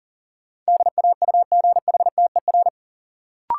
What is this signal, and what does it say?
Morse code, heard as a keyed tone, about 40 wpm, 700 Hz. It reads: DAUGHTER
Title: Sentences from Top 1500 Words - New Word Drill Alt - 40wpm